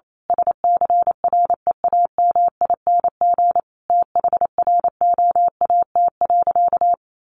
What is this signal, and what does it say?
Morse code, heard as a keyed tone, about 28 wpm, 700 Hz. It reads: SCREAMING THROAT.
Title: A Sound Of Thunder 28wpm